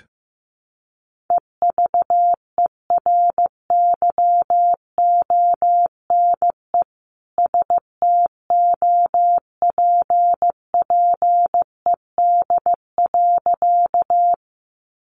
Morse code, heard as a keyed tone, about 15 wpm, 700 Hz. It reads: EVERYONE STOPPED.
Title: A Sound Of Thunder 15wpm